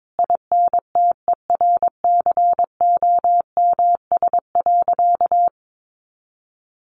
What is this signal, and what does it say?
Morse code, heard as a keyed tone, about 22 wpm, 700 Hz. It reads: INTERCOMS.
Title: A Sound Of Thunder 22wpm